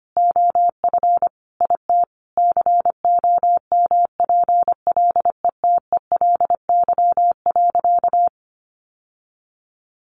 Morse code, heard as a keyed tone, about 25 wpm, 700 Hz. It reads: OF IT COMPLETELY.
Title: Sentences from Top 1100 Words 25wpm